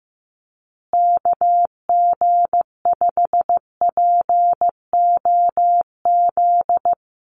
Morse code, heard as a keyed tone, about 15 wpm, 700 Hz. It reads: KG5POZ
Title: USA Call Signs Repeated 2x 15wpm